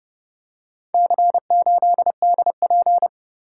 Morse code, heard as a keyed tone, about 29 wpm, 700 Hz. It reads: C8DP